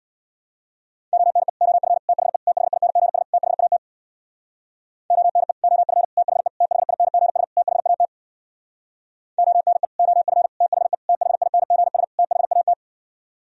Morse code, heard as a keyed tone, about 59 wpm, 700 Hz. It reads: ONE OF THE THINGS THAT ONE OF THE THINGS THAT ONE OF THE THINGS THAT
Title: Speed-Racing - Sets of 5 Words 50wpm